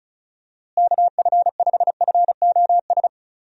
Morse code, heard as a keyed tone, about 34 wpm, 700 Hz. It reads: KF5FOS
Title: USA Call Signs - 6 Characters - Repeated 3x - 35wpm